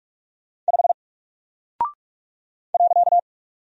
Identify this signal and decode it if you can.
Morse code, heard as a keyed tone, about 44 wpm, 700 Hz. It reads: 5 .
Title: Single Character (Letter:Number:Punctuation:Prosign) 45wpm